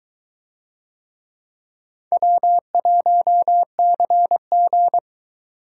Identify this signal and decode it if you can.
Morse code, heard as a keyed tone, about 22 wpm, 700 Hz. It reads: W1CG